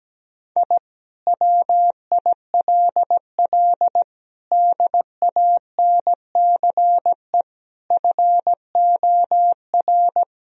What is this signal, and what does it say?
Morse code, heard as a keyed tone, about 17 wpm, 700 Hz. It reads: I WILL DANCE FOR
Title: Sentences from Top 1000 Words - Review Part 4 - 17wpm